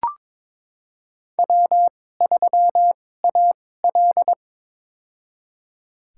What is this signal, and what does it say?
Morse code, heard as a keyed tone, about 22 wpm, spaced as for 16 wpm, 700 Hz. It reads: W3AL
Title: USA Call Signs - 4 Characters - 2x Character Spacing - 22wpm